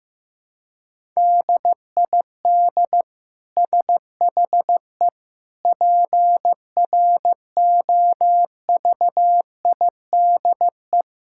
Morse code, heard as a keyed tone, about 15 wpm, 700 Hz. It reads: DID SHE PROVIDE
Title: Sentences from Top 1000 Words - Review Part 5 - 15wpm